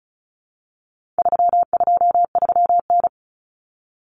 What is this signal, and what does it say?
Morse code, heard as a keyed tone, about 34 wpm, 700 Hz. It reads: 323N
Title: Common Cut Numbers - 4-Digit 35wpm